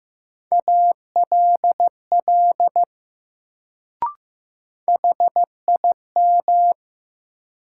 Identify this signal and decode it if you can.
Morse code, heard as a keyed tone, about 15 wpm, 700 Hz. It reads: ALL HIM
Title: Sets of 2 Words - New Word Drill - 15wpm